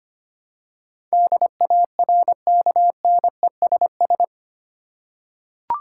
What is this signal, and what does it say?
Morse code, heard as a keyed tone, about 25 wpm, 700 Hz. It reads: DARKNESS
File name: 8 Letter Words - First 3 Letters Given - 25wpm